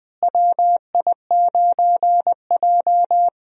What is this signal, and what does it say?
Morse code, heard as a keyed tone, about 20 wpm, 700 Hz. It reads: WI9J
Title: USA Call Signs - 4 Characters - Repeated 3x - 20wpm